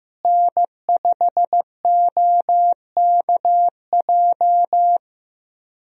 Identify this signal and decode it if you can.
Morse code, heard as a keyed tone, about 15 wpm, 700 Hz. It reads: N5OKJ
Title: USA Call Signs - 5 Characters - Repeated 2x - 15wpm